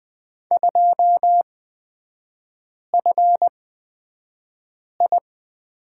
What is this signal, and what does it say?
Morse code, heard as a keyed tone, about 20 wpm, spaced as for 5 wpm, 700 Hz. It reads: 2FI
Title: 243 Lesson 39e - Letter z - Three Characters 20wpm